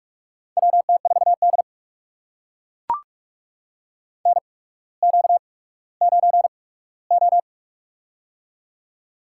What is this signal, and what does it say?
Morse code, heard as a keyed tone, about 44 wpm, spaced as for 19 wpm, 700 Hz. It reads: WT4D NQ9O
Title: USA Call Signs - 4 Characters - 8x Character Spacing - 45wpm